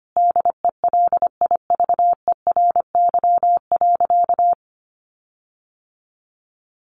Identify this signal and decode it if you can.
Morse code, heard as a keyed tone, about 25 wpm, 700 Hz. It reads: DELIVERY.
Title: Sentences from Top 1500 Words 25wpm